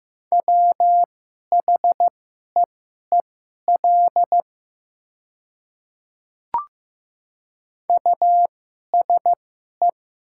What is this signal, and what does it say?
Morse code, heard as a keyed tone, about 15 wpm, spaced as for 11 wpm, 700 Hz. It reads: WHEEL USE
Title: 2x Character Spacing - Top 500 Words - 15wpm